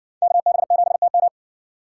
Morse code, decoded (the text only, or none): KB7TM